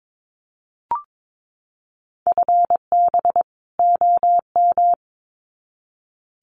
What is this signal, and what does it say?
Morse code, heard as a keyed tone, about 22 wpm, 700 Hz. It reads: FB OM
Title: Common QSO Phrases 22wpm